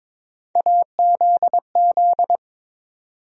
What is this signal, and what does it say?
Morse code, heard as a keyed tone, about 22 wpm, 700 Hz. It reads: AZZ
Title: USA Call Sign Postfixes - 3 Letters 22wpm